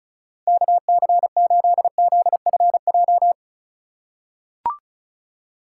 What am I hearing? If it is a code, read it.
Morse code, heard as a keyed tone, about 36 wpm, 700 Hz. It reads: KC8ZFJ
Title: USA Call Signs - 6 Characters - 4x Character Spacing - 35wpm